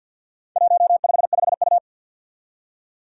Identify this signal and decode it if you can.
Morse code, heard as a keyed tone, about 51 wpm, 700 Hz. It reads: 155U
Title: Four Challenging Letter-Number Combinations 50wpm